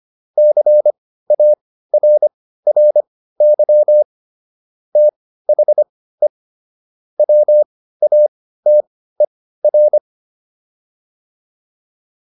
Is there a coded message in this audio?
Yes: Morse code from 0.4 to 10.0 s, heard as a keyed tone, about 25 wpm, spaced as for 15 wpm, 600 Hz. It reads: CARRY THE WATER